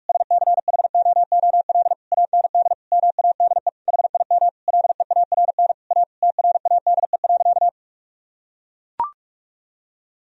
Morse code, heard as a keyed tone, about 41 wpm, 700 Hz. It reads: SCHOOL AND MADE HIM LEARN A TRADE.
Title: Sentences from Top 1000 Words - Review Part 4 - 45wpm